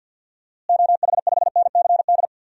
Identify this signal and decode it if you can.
Morse code, heard as a keyed tone, about 52 wpm, 700 Hz. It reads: OH5NCD
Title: DXCC-100 Call Signs - 6 Characters - Repeated 3x - 50wpm